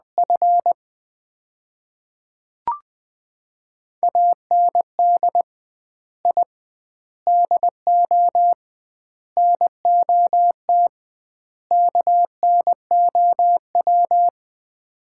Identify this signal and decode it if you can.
Morse code, heard as a keyed tone, about 20 wpm, 700 Hz. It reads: F AND I DO NOT KNOW
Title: 2x Word-Spacing - Sets of 5 Words - 20wpm